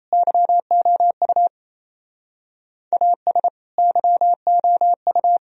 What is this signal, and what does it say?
Morse code, heard as a keyed tone, about 34 wpm, 700 Hz. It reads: YOU AS YOU
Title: Speed-Racing - Sets of 2 Words 28wpm